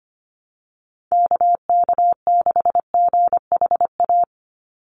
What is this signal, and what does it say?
Morse code, heard as a keyed tone, about 25 wpm, 700 Hz. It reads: KK6GHA
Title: Call Signs - 25wpm